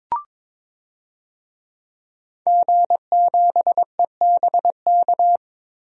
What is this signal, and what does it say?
Morse code, heard as a keyed tone, about 22 wpm, 700 Hz. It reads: G7EBK
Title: DXCC-100 Call Signs - 5 Characters - Repeated 2x - 22wpm